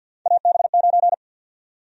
Morse code, heard as a keyed tone, about 49 wpm, 700 Hz. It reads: AB9
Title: USA Call Sign Prefixes - 50wpm